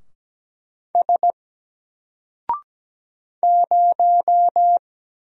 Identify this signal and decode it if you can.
Morse code, heard as a keyed tone, about 17 wpm, 700 Hz. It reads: S 0
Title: Single Letter-Number 17wpm